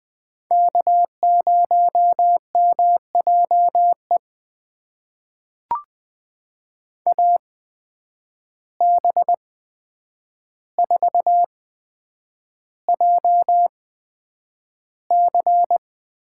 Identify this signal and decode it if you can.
Morse code, heard as a keyed tone, about 20 wpm, spaced as for 9 wpm, 700 Hz. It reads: K0MJE AB4JC
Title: USA Call Signs - 5 Characters - 8x Character Spacing - 20wpm